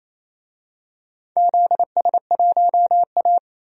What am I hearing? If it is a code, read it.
Morse code, heard as a keyed tone, about 28 wpm, 700 Hz. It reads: ZS1A
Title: DXCC-100 Call Signs - 4 Characters - 28wpm